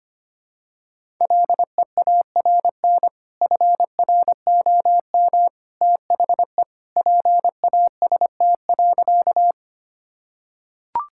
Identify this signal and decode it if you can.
Morse code, heard as a keyed tone, about 25 wpm, 700 Hz. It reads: LEARN FROM THE PAST.
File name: practice / 25wpm / Sentences with 2-Word Phrases - 25wpm